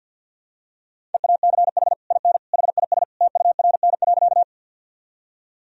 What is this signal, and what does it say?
Morse code, heard as a keyed tone, about 47 wpm, 700 Hz. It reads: EACH IN HIS TURN.
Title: Sentences from Top 1000 Words - Review Part 1 - 50wpm